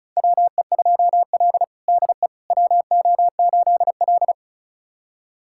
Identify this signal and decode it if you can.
Morse code, heard as a keyed tone, about 36 wpm, 700 Hz. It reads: WE2L DE WO8L
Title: USA Callsign DE Callsign - 4 Characters - 35wpm